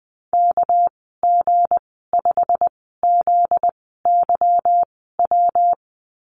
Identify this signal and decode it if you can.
Morse code, heard as a keyed tone, about 20 wpm, spaced as for 14 wpm, 700 Hz. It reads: KG5ZYW
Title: 245 Lesson 39g - Letter z - Call Signs 20wpm